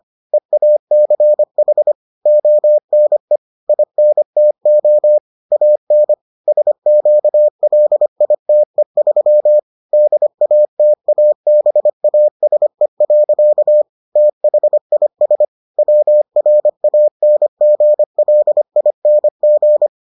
Morse code, heard as a keyed tone, about 25 wpm, 600 Hz. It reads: EACH ONE INTO AN SQLITE3 DATABASE. THIS WRANGLING